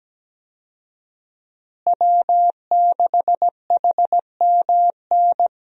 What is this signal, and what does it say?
Morse code, heard as a keyed tone, about 17 wpm, 700 Hz. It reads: W6HMN
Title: USA Call Signs - 5 Characters - Repeated 3x - 17wpm